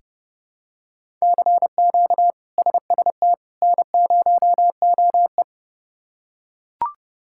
Morse code, heard as a keyed tone, about 30 wpm, 700 Hz. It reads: CQ SST N0OE
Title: K1USN SST QSO Practice - 30wpm